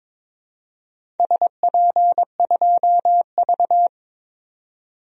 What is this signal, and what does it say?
Morse code, heard as a keyed tone, about 22 wpm, 700 Hz. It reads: SP2V
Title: DXCC-100 Call Signs - 4 Characters - 22wpm